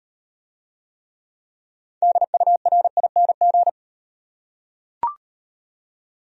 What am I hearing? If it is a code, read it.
Morse code, heard as a keyed tone, about 36 wpm, 700 Hz. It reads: DURING